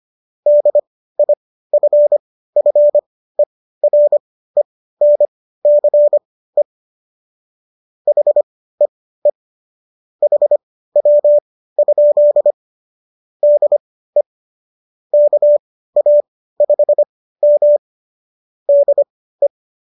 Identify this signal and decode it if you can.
Morse code, heard as a keyed tone, about 25 wpm, spaced as for 15 wpm, 590 Hz. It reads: DIFFERENCE HEE HW? DE KA5M DE